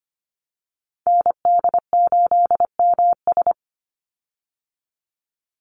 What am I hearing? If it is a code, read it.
Morse code, heard as a keyed tone, about 25 wpm, 700 Hz. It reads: ND8MS
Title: Call Signs Repeated 3x - 25wpm